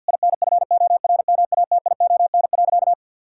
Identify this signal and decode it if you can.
Morse code, heard as a keyed tone, about 50 wpm, 700 Hz. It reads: INFORMATION.
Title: Sentences from Top 1000 Words - Review Part 6 - 50wpm